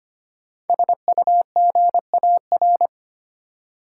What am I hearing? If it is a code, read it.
Morse code, heard as a keyed tone, about 25 wpm, 700 Hz. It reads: SUGAR
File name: 5 Letter Words - First 3 Letters Given - 25wpm